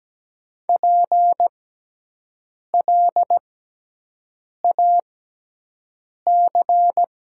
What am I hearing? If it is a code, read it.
Morse code, heard as a keyed tone, about 17 wpm, spaced as for 6 wpm, 700 Hz. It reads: PLAC
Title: 6x Character Spacing - Top 300 Words - 17wpm